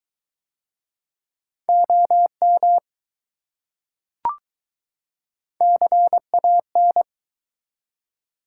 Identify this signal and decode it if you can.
Morse code, heard as a keyed tone, about 20 wpm, 700 Hz. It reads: OM CAN